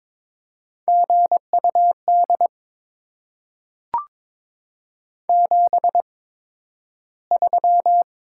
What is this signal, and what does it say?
Morse code, heard as a keyed tone, about 22 wpm, spaced as for 12 wpm, 700 Hz. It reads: GUD 73
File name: Top 20 QSO Elements - 8x Character Spacing - 22wpm